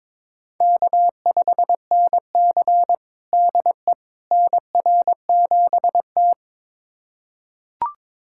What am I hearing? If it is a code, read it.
Morse code, heard as a keyed tone, about 23 wpm, 700 Hz. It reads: K5NC DE NR7T
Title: USA Callsign DE Callsign - 4 Characters - 22wpm